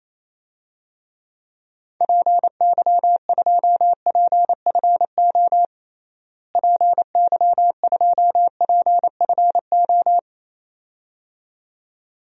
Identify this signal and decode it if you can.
Morse code, heard as a keyed tone, about 28 wpm, 700 Hz. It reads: PY2PFO PY2PFO
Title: DXCC-100 Call Signs - 6 Characters - Repeated 2x - 28wpm